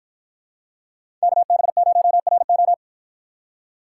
Morse code, heard as a keyed tone, about 51 wpm, 700 Hz. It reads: KB0RO